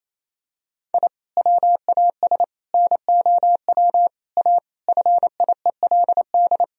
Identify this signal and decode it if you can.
Morse code, heard as a keyed tone, about 28 wpm, 700 Hz. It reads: I WAS NOW A FIELD
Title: Sentences from Top 1000 Words - Review Part 2 - 28wpm